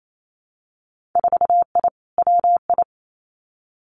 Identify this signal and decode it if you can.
Morse code, heard as a keyed tone, about 28 wpm, 700 Hz. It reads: 4I WI
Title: Winter Field Day Exchanges 28wpm